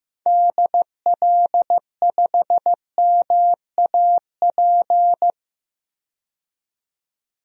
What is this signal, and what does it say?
Morse code, heard as a keyed tone, about 15 wpm, 700 Hz. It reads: DL5MAP
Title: DXCC-100 Call Signs - 6 Characters - 15wpm